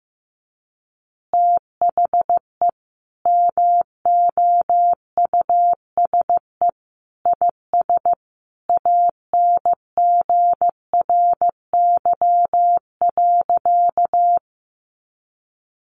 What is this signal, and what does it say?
Morse code, heard as a keyed tone, about 15 wpm, 700 Hz. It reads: THE MOUSE IS ANGRY.
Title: Easy Phrases 15wpm